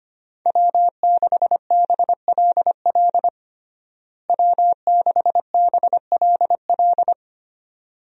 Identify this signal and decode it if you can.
Morse code, heard as a keyed tone, about 25 wpm, 700 Hz. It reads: W6BLL W6BLL
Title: USA Call Signs - 5 Characters - Repeated 2x - 25wpm